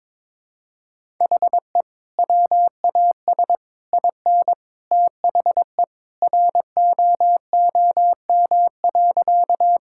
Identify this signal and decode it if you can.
Morse code, heard as a keyed tone, about 22 wpm, 700 Hz. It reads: HE WAS IN THE ROOM.
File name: practice / 22wpm / Sentences with 2-Word Phrases - 22wpm